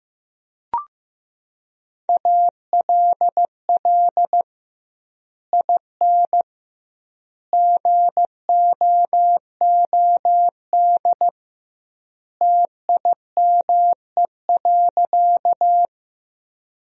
Morse code, heard as a keyed tone, about 15 wpm, 700 Hz. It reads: ALL IN GOOD TIME.
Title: 2x Word-Spacing - Sentences from Top 200 Words - 15wpm